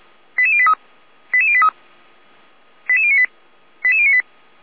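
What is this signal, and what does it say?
Five-tone selective call (ZVEI), heard as quick runs of five tones, tones of about 70 ms. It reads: 88083 88083 88098 88098
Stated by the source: ZVEI five-tone selective call received by SDR, from the feuerwehr-orlamuende.org fire brigade server, address 88141.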